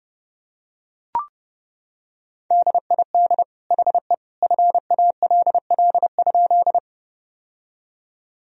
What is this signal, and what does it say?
Morse code, heard as a keyed tone, about 30 wpm, 700 Hz. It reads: DID HE FALL?
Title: Sentences from Top 1000 Words - Review Part 2 - 30wpm